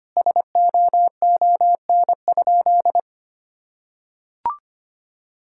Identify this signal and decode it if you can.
Morse code, heard as a keyed tone, about 25 wpm, 700 Hz. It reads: SOON?
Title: Sentences from Top 1600 Words 25wpm